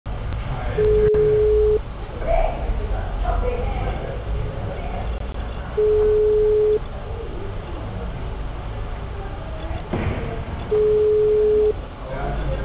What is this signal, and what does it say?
Ringback tone (Europe, 425 Hz), with speech.